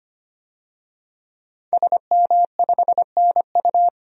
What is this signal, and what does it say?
Morse code, heard as a keyed tone, about 25 wpm, 700 Hz. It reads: SM5NU
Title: DXCC-100 Call Signs - 5 Characters - 25wpm